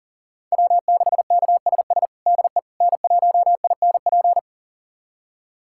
Morse code, heard as a keyed tone, about 40 wpm, 700 Hz. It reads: W6KSS DE N1INP
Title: USA Callsign DE Callsign - 5 Characters - 40wpm